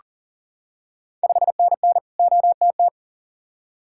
Morse code, heard as a keyed tone, about 39 wpm, 690 Hz. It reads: 5NN OTT
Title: Ontario QSO Party Exchange 40wpm